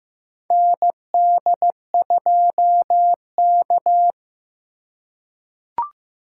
Morse code, heard as a keyed tone, about 15 wpm, 700 Hz. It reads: ND2K
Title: USA Call Signs - 4 Characters - 3x Character Spacing - 15wpm